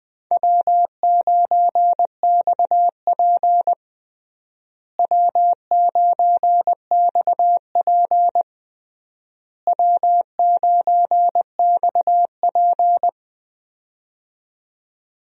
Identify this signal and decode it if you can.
Morse code, heard as a keyed tone, about 21 wpm, 700 Hz. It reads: W9XP W9XP W9XP
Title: USA Call Signs - 4 Characters - Repeated 3x - 20wpm